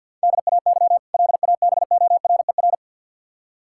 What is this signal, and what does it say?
Morse code, heard as a keyed tone, about 52 wpm, 700 Hz. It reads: DAY LABORER
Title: K1USN SST QSO Practice 50wpm